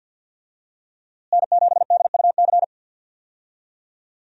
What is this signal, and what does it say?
Morse code, heard as a keyed tone, about 50 wpm, 700 Hz. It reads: N7DUC
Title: USA Call Signs - 5 Characters - Repeated 3x - 50wpm